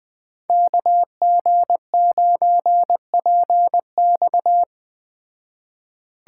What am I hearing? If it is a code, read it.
Morse code, heard as a keyed tone, about 20 wpm, 700 Hz. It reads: KG9PX
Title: Speed Racing - USA 2x2 Call Signs 17wpm